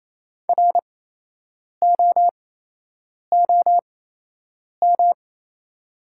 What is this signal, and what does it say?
Morse code, heard as a keyed tone, about 28 wpm, spaced as for 8 wpm, 700 Hz. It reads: ROOM
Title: 8x Character Spacing - Top 300 Words - 28wpm